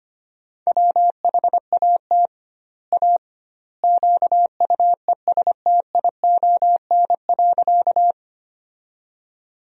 Morse code, heard as a keyed tone, about 25 wpm, 700 Hz. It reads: WHAT A QUESTION.
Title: 2x Word-Spacing - Sentences from Top 400 Words - 25wpm